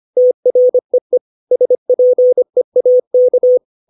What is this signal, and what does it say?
Morse code, heard as a keyed tone, about 25 wpm, 500 Hz. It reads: TREE SPEAK
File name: 632 words / 632 Words 25WPM